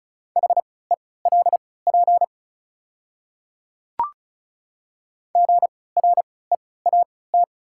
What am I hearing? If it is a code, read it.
Morse code, heard as a keyed tone, about 35 wpm, spaced as for 20 wpm, 700 Hz. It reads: HELP GREAT
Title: 3x Character Spacing - Top 200 Words - 35wpm